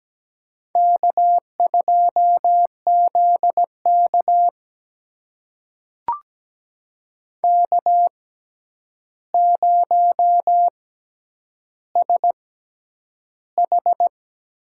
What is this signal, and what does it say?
Morse code, heard as a keyed tone, about 17 wpm, spaced as for 9 wpm, 700 Hz. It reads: K2ZK K0SH
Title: USA Call Signs - 4 Characters - 6x Character Spacing - 17wpm